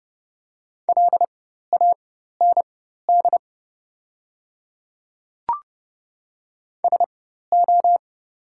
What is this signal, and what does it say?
Morse code, heard as a keyed tone, about 30 wpm, spaced as for 14 wpm, 700 Hz. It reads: LAND SO
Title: 4x Character Spacing - Top 200 Words - 30wpm